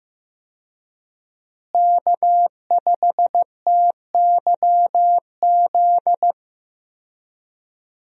Morse code, heard as a keyed tone, about 15 wpm, 700 Hz. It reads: K5TYZ
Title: USA Call Signs - 5 Characters - Repeated 3x - 15wpm